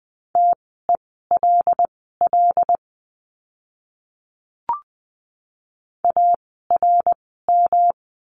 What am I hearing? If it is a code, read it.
Morse code, heard as a keyed tone, about 20 wpm, spaced as for 14 wpm, 700 Hz. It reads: TELL ARM
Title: 108 Lesson 18f - Letter m - Words 20wpm